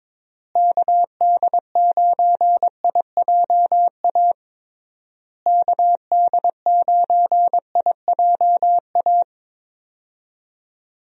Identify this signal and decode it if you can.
Morse code, heard as a keyed tone, about 22 wpm, 700 Hz. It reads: KD9IJA KD9IJA
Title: USA Call Signs - 6 Characters - Repeated 3x - 22wpm